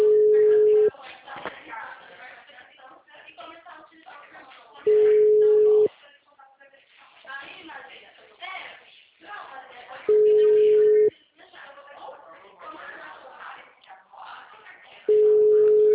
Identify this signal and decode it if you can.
Ringback tone (Europe, 425 Hz), with speech.
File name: out- -2001- - - .29574